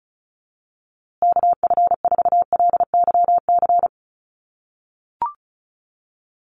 Morse code, heard as a keyed tone, about 34 wpm, 700 Hz. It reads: KF4LYC
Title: USA Call Signs Repeated 2x 35wpm